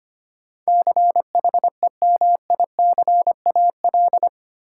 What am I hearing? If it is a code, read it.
Morse code, heard as a keyed tone, about 25 wpm, 700 Hz. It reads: CHEMICAL
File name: 8 Letter Words - First 3 Letters Given - 25wpm